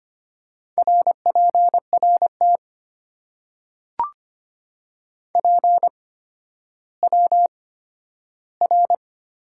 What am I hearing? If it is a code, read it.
Morse code, heard as a keyed tone, about 25 wpm, spaced as for 12 wpm, 700 Hz. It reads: RPRT PWR
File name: Top 90 QSO Elements - 8x Character Spacing - 25wpm